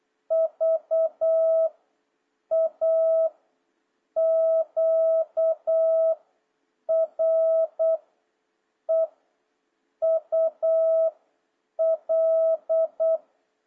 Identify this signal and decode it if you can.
Morse code, heard as a keyed tone, about 8 wpm, 650 Hz. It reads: V A Q R E U L
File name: jh8cbh / 2026cw